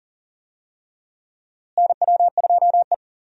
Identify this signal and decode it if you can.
Morse code, heard as a keyed tone, about 40 wpm, 700 Hz. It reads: NW2E
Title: USA Call Signs - 4 Characters - Repeated 3x - 40wpm